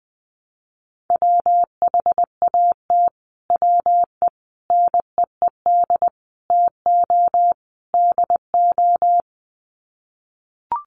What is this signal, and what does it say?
Morse code, heard as a keyed tone, about 20 wpm, 700 Hz. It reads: WHAT WE NEED TO DO
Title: Sets of 5 Words Encore 20wpm